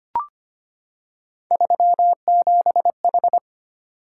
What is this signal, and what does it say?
Morse code, heard as a keyed tone, about 25 wpm, 700 Hz. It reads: 37H
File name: Three Letter-Number Combinations - 25wpm